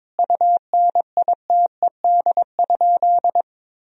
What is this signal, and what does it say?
Morse code, heard as a keyed tone, about 21 wpm, 700 Hz. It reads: UNITED?
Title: Sentences from Top 1000 Words - Review Part 4 - 22wpm